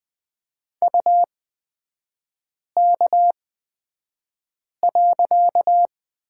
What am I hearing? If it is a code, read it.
Morse code, heard as a keyed tone, about 20 wpm, spaced as for 5 wpm, 700 Hz. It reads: UK.
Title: 255 Lesson 41e - Period - Three Characters 20wpm